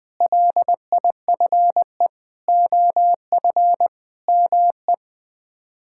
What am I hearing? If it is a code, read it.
Morse code, heard as a keyed tone, about 20 wpm, 700 Hz. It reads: LIFE OF ME
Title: Sets of 8 Words - 20wpm